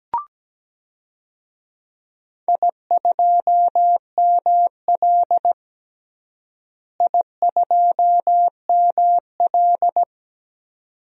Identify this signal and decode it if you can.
Morse code, heard as a keyed tone, about 17 wpm, 700 Hz. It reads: I2ML I2ML
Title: DXCC-100 Call Signs - 4 Characters - Repeated 2x - 17wpm